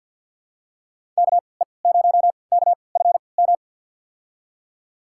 Morse code, heard as a keyed tone, about 52 wpm, spaced as for 29 wpm, 700 Hz. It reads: KE0KFM
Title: USA Call Signs - 6 Characters - 3x Character Spacing - 50wpm